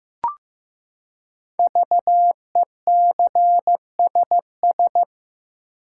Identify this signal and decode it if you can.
Morse code, heard as a keyed tone, about 15 wpm, 700 Hz. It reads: VECSS
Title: Code Groups - 5 Letters - 15wpm